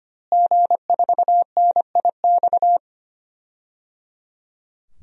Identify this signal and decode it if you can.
Morse code, heard as a keyed tone, about 25 wpm, 700 Hz. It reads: G4NIX
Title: DXCC-100 Call Signs - 5 Characters - Repeated 3x - 25wpm